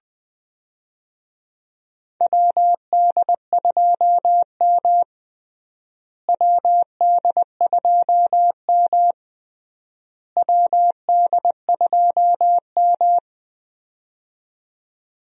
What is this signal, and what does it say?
Morse code, heard as a keyed tone, about 21 wpm, 700 Hz. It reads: WD2M WD2M WD2M
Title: USA Call Signs - 4 Characters - Repeated 3x - 20wpm